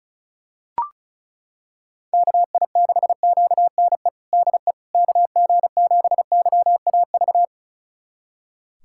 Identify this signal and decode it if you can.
Morse code, heard as a keyed tone, about 36 wpm, 690 Hz. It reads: KI6QNE DE KG7YAV
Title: USA Callsign DE Callsign 35wpm